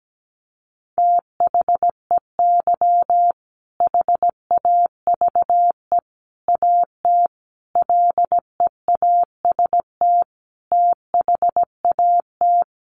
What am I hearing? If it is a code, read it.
Morse code, heard as a keyed tone, about 17 wpm, 700 Hz. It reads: THEY HAVE AT LEAST THAT
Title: Sentences from Top 900 Words 17wpm